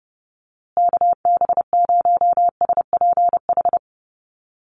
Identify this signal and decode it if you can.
Morse code, heard as a keyed tone, about 30 wpm, 700 Hz. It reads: KB0SPH
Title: USA Call Signs Repeated 2x 30wpm